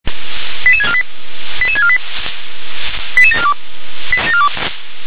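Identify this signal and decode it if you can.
Five-tone selective call (ZVEI), heard as quick runs of five tones, tones of about 70 ms. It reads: 88648 88648 88742
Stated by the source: ZVEI five-tone selective call received by SDR.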